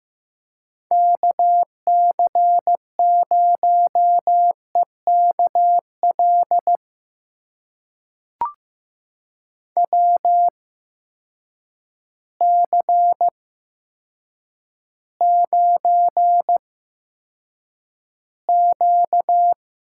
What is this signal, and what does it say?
Morse code, heard as a keyed tone, about 15 wpm, spaced as for 7 wpm, 700 Hz. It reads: KC0EK L WC9Q
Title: USA Call Signs - 8x Character Spacing - 15wpm